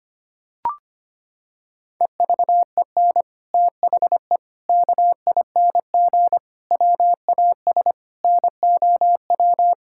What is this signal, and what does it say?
Morse code, heard as a keyed tone, about 25 wpm, 700 Hz. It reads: EVEN THE KING WAS NOW
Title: Sentences from Top 1000 Words - Review Part 6 - 25wpm